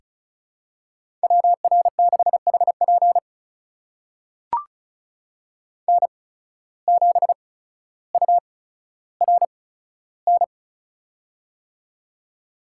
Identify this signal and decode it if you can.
Morse code, heard as a keyed tone, about 36 wpm, spaced as for 15 wpm, 700 Hz. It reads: WR6HP N7URN
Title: USA Call Signs - 5 Characters - 8x Character Spacing - 35wpm